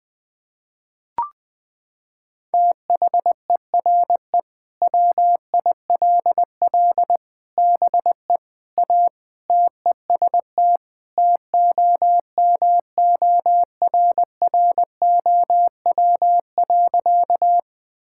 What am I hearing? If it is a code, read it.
Morse code, heard as a keyed tone, about 20 wpm, 700 Hz. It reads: THERE WILL BE A TEST TOMORROW.
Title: Sentences with 3-Word Phrases - 20wpm